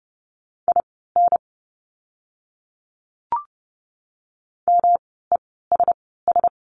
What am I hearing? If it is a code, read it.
Morse code, heard as a keyed tone, about 30 wpm, spaced as for 17 wpm, 700 Hz. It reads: IN MESS
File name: course / 30:10wpm / 108 Lesson 18f - Letter m - Words 30wpm